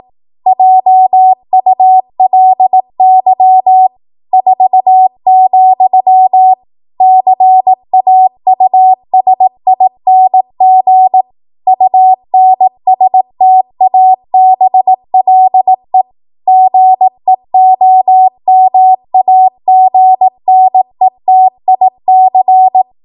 Morse code, heard as a keyed tone, about 18 wpm, 750 Hz. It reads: JULY 4, CAUSING UNSTABLE GEOMAGNETIC
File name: Morse / 18WPM